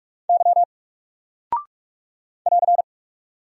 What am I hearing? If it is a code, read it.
Morse code, heard as a keyed tone, about 44 wpm, 700 Hz. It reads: Y <AR>
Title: Single Character (Letter:Number:Punctuation:Prosign) Alt 45wpm